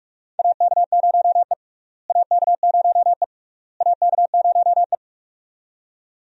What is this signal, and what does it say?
Morse code, heard as a keyed tone, about 46 wpm, 700 Hz. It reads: AK0E AK0E AK0E
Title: USA Call Signs - 4 Characters - Repeated 3x - 45wpm